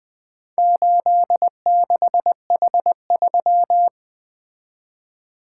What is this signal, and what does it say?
Morse code, heard as a keyed tone, about 20 wpm, 700 Hz. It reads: 86H3
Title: Four Letter-Number Combinations - 20wpm